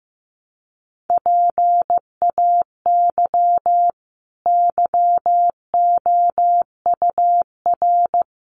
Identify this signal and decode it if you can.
Morse code, heard as a keyed tone, about 15 wpm, 700 Hz. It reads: PAY YOUR
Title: Sentences from Top 900 Words 15wpm